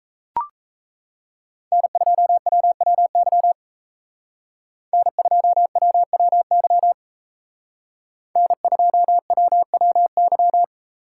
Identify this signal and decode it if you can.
Morse code, heard as a keyed tone, about 36 wpm, 700 Hz. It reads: N2WWY N2WWY N2WWY